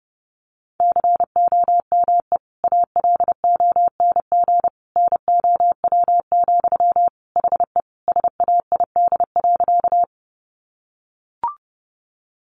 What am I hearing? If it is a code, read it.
Morse code, heard as a keyed tone, about 30 wpm, 700 Hz. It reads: COME ALONG NOW, HE SAID.
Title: Sentences from Top 700 Words 30wpm